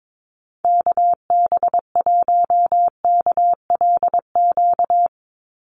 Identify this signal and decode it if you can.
Morse code, heard as a keyed tone, about 22 wpm, 700 Hz. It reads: KB1KLQ
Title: USA Call Signs 22wpm